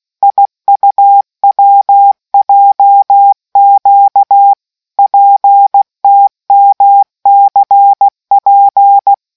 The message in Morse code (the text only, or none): IUWJQ PTMCP